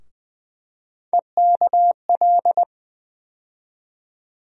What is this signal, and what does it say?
Morse code, heard as a keyed tone, about 20 wpm, 700 Hz. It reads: EKL
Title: USA Call Sign Postfixes - 3 Letters 20wpm